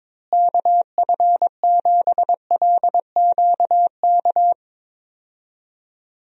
Morse code, heard as a keyed tone, about 22 wpm, 700 Hz. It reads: KF7LQK
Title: USA Call Signs - 6 Characters - Repeated 3x - 22wpm